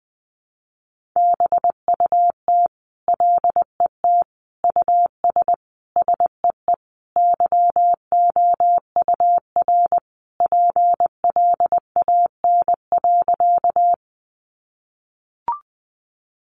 Morse code, heard as a keyed tone, about 20 wpm, 700 Hz. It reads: BUT LET US SEE YOUR PLAN.
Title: Sentences from Top 500 Words 20wpm